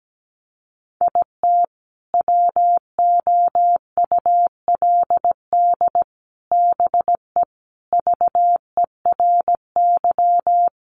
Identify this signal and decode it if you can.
Morse code, heard as a keyed tone, about 17 wpm, 700 Hz. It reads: IT WOULD BE VERY
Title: Sentences from Top 1100 Words 17wpm